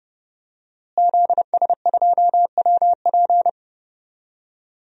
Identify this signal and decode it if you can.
Morse code, heard as a keyed tone, about 32 wpm, 700 Hz. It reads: ZS2WP
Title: DXCC-100 Call Signs - 5 Characters - Repeated 3x - 30wpm